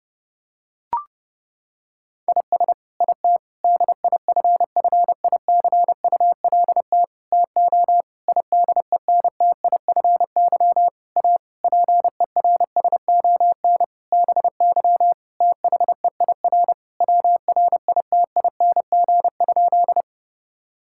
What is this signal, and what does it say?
Morse code, heard as a keyed tone, about 32 wpm, 700 Hz. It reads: IS IT DIFFICULT TO IDENTIFY A PERSON BY THEIR WRITING?
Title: Sentences from Top 1000 Words - Review Part 6 - 30wpm